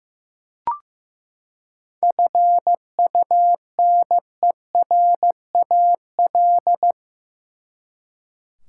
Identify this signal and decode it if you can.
Morse code, heard as a keyed tone, about 15 wpm, 700 Hz. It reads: FUNERAL
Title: Seven Letter Words - 15wpm